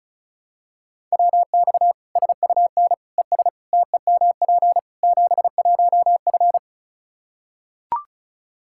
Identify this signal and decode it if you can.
Morse code, heard as a keyed tone, about 34 wpm, 700 Hz. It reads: WX SUN ES TEMP 71F
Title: Practice QSO - Exchange 2 - Info - 35wpm